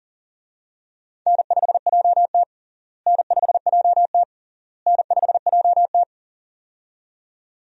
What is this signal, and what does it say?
Morse code, heard as a keyed tone, about 40 wpm, 700 Hz. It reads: N5JT N5JT N5JT
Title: USA Call Signs - 4 Characters - Repeated 3x - 40wpm